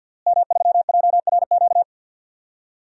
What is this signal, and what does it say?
Morse code, heard as a keyed tone, about 52 wpm, 700 Hz. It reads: M3JRQ
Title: DXCC-100 Call Signs - 5 Characters - Repeated 2x - 50wpm